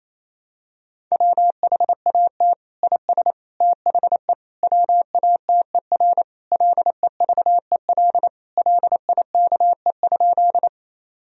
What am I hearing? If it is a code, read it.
Morse code, heard as a keyed tone, about 28 wpm, 700 Hz. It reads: WHAT IS THE WATER LEVEL LIKE?
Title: Sentences from Top 1000 Words - Review Part 4 - 28wpm